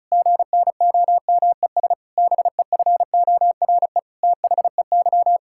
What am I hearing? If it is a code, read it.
Morse code, heard as a keyed tone, about 36 wpm, 700 Hz. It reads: GNOMES BEFORE THEY
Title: Pangram Challenge v2 - 35wpm